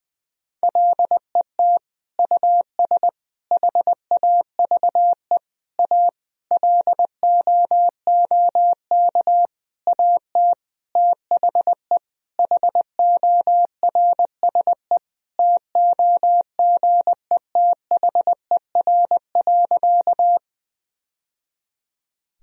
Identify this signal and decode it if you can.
Morse code, heard as a keyed tone, about 19 wpm, 700 Hz. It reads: LET US HAVE A LOOK AT THE HORSE TOGETHER.
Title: Sentences from Top 1000 Words - Review Part 1 - 20wpm